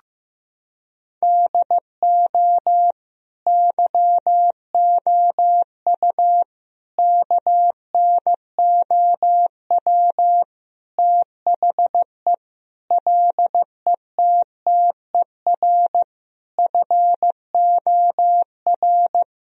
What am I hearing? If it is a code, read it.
Morse code, heard as a keyed tone, about 16 wpm, 700 Hz. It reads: DO YOU KNOW THE LETTER FOR
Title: Sentences from Top 1000 Words - Review Part 2 - 15wpm